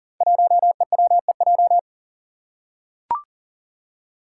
Morse code, heard as a keyed tone, about 40 wpm, 700 Hz. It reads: 1EWEJ
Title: Five Challenging Letter-Number Combinations 40wpm